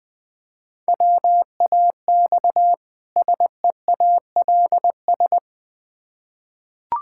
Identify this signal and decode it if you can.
Morse code, heard as a keyed tone, about 20 wpm, 700 Hz. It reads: WAX SEALS
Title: Difficult Words with Hints - 20wpm